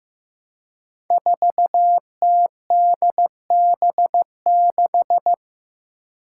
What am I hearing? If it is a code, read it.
Morse code, heard as a keyed tone, about 15 wpm, 700 Hz. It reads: 4TDB6
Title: Five Challenging Letter-Number Combinations 15wpm